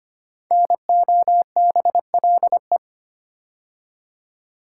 Morse code, heard as a keyed tone, about 25 wpm, 700 Hz. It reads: NOBLE
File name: 5 Letter Words - First 3 Letters Given - 25wpm